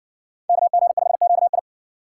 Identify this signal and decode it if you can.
Morse code, heard as a keyed tone, about 54 wpm, 700 Hz. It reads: KG5QI